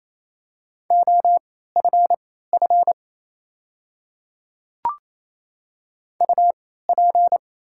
Morse code, heard as a keyed tone, about 28 wpm, spaced as for 16 wpm, 700 Hz. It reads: OFF UP
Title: 3x Character Spacing - Top 200 Words - 28wpm